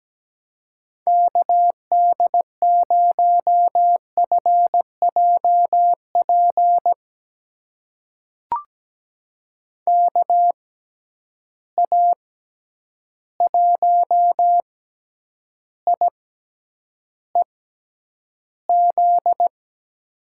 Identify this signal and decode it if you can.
Morse code, heard as a keyed tone, about 17 wpm, spaced as for 9 wpm, 700 Hz. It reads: KD0FJP KA1IEZ
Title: USA Call Signs - 6 Characters - 6x Character Spacing - 17wpm